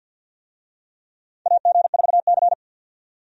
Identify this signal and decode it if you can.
Morse code, heard as a keyed tone, about 48 wpm, 700 Hz. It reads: AG4C
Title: USA Call Signs - 4 Characters - 50wpm